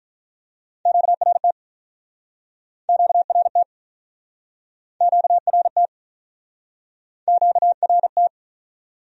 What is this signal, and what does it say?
Morse code, heard as a keyed tone, about 50 wpm, 700 Hz. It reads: QRT QRT QRT QRT